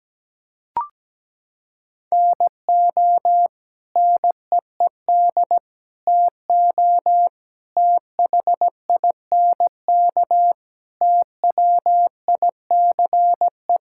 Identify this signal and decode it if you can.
Morse code, heard as a keyed tone, about 17 wpm, 700 Hz. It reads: NO NEED TO THINK TWICE
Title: Sentences from Top 1600 Words 17wpm